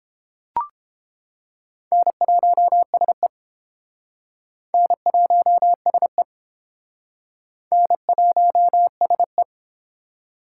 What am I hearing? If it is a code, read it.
Morse code, heard as a keyed tone, about 28 wpm, 700 Hz. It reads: N1SE N1SE N1SE